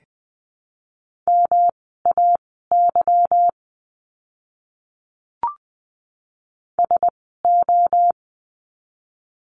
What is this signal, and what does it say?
Morse code, heard as a keyed tone, about 20 wpm, spaced as for 15 wpm, 700 Hz. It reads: MAY SO
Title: 244 Lesson 39f - Letter z - Words 20wpm